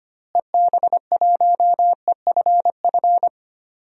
Morse code, heard as a keyed tone, about 25 wpm, 700 Hz. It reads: EB1EFF
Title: DXCC-100 Call Signs - 6 Characters - Repeated 3x - 25wpm